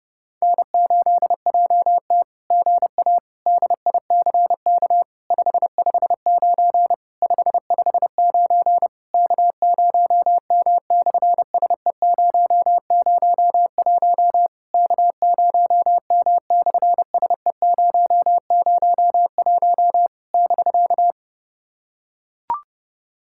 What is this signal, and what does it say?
Morse code, heard as a keyed tone, about 30 wpm, 700 Hz. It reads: N8JT GA DICK 559 559 K0M/SE001 K0M/SE001 <BK>
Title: SOTA QSO Practice 30wpm